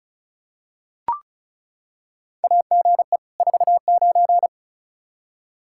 Morse code, heard as a keyed tone, about 34 wpm, 700 Hz. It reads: AGE 49
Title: Practice QSO - Exchange 3 - More Info - 35wpm